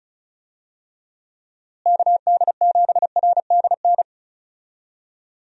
Morse code, heard as a keyed tone, about 36 wpm, 700 Hz. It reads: KD7RDN
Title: USA Call Signs - 6 Characters - Repeated 3x - 35wpm